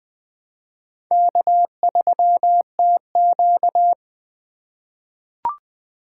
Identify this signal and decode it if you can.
Morse code, heard as a keyed tone, about 21 wpm, 700 Hz. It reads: K3TQ
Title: USA Call Signs - 4 Characters - 2x Character Spacing - 20wpm